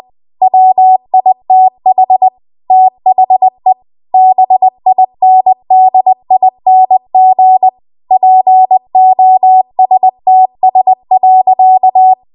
Morse code, heard as a keyed tone, about 20 wpm, 750 Hz. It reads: WITH THE BINDING POSTS.
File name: Morse / 20WPM